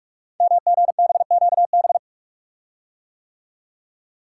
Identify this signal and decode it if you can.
Morse code, heard as a keyed tone, about 44 wpm, 700 Hz. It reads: MGBQB
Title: Code Groups - 5 Letters - 45wpm